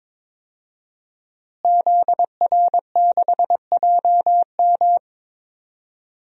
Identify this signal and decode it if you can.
Morse code, heard as a keyed tone, about 21 wpm, 700 Hz. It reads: ZR6JM
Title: DXCC-100 Call Signs - 5 Characters - Repeated 2x - 22wpm